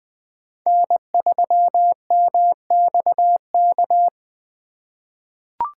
Morse code, heard as a keyed tone, about 20 wpm, 700 Hz. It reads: N3MXK
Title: USA Call Signs - 5 Characters - 2x Character Spacing - 20wpm